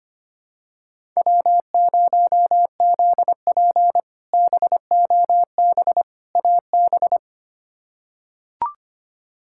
Morse code, heard as a keyed tone, about 25 wpm, 700 Hz. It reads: W0ZP BOB AB
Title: K1USN SST QSO Practice - 25wpm